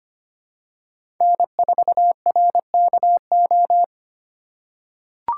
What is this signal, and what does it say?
Morse code, heard as a keyed tone, about 25 wpm, 700 Hz. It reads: N4RKO
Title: USA Call Signs - 5 Characters - 2x Character Spacing - 25wpm